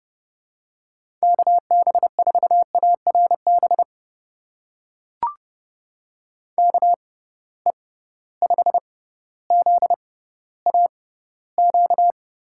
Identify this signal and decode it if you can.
Morse code, heard as a keyed tone, about 30 wpm, spaced as for 15 wpm, 700 Hz. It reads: KB4ARB KE5ZAQ
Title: USA Call Signs - 6x Character Spacing - 30wpm